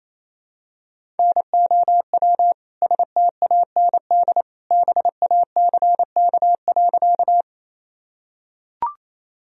Morse code, heard as a keyed tone, about 28 wpm, 700 Hz. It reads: NOW STAND BACK.
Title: Sentences from Top 1000 Words - Review Part 1 - 28wpm